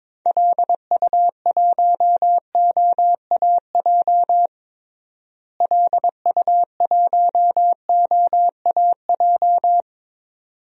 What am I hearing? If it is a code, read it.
Morse code, heard as a keyed tone, about 22 wpm, 700 Hz. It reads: LU1OAJ LU1OAJ
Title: DXCC-100 Call Signs - 6 Characters - Repeated 2x - 22wpm